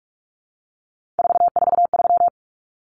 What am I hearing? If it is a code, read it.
Morse code, heard as a keyed tone, about 44 wpm, 700 Hz. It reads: 443
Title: Numbers - 3-Digit 45wpm